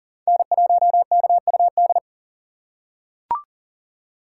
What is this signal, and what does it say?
Morse code, heard as a keyed tone, about 40 wpm, 700 Hz. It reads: N1KUD
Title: USA Call Signs - 5 Characters - 3x Character Spacing - 40wpm